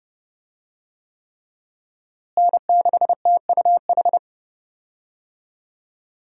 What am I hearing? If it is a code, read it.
Morse code, heard as a keyed tone, about 31 wpm, 700 Hz. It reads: N6TUH
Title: USA Call Signs - 5 Characters - Repeated 3x - 30wpm